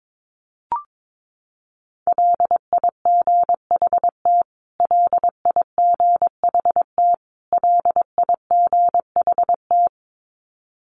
Morse code, heard as a keyed tone, about 22 wpm, 700 Hz. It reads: LIGHT LIGHT LIGHT
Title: Top 300 Words Repeated 22wpm